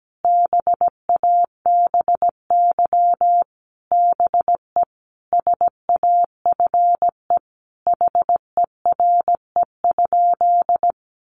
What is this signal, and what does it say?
Morse code, heard as a keyed tone, about 17 wpm, 700 Hz. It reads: BABY BE SAFE HERE?
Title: Sentences from Top 1300 Words 17wpm